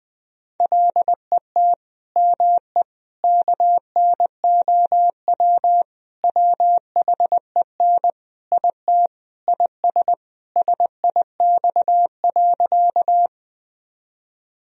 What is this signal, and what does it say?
Morse code, heard as a keyed tone, about 20 wpm, 700 Hz. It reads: LET ME KNOW WHEN IT IS SIX.
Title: Sentences from Top 1000 Words - Review Part 2 - 20wpm